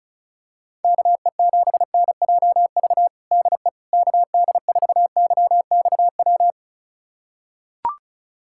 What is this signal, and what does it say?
Morse code, heard as a keyed tone, about 36 wpm, 700 Hz. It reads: KE7NJV DE KD4YXW
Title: USA Callsign DE Callsign 35wpm